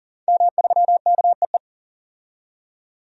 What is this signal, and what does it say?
Morse code, heard as a keyed tone, about 40 wpm, 700 Hz. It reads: M3KEE
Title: DXCC-100 Call Signs - 5 Characters - 40wpm